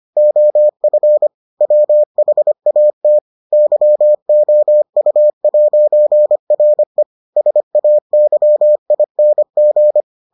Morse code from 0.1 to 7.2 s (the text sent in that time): OF WHAT YOU'RE